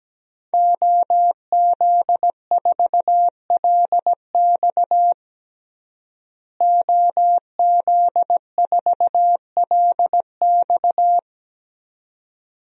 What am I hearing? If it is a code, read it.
Morse code, heard as a keyed tone, about 17 wpm, 700 Hz. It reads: OZ4LX OZ4LX
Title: DXCC-100 Call Signs - 5 Characters - Repeated 2x - 17wpm